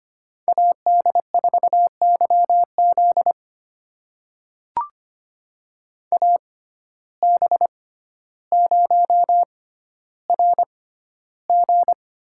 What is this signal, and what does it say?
Morse code, heard as a keyed tone, about 25 wpm, spaced as for 13 wpm, 700 Hz. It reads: AD4YZ AB0RG
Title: USA Call Signs - 5 Characters - 6x Character Spacing - 25wpm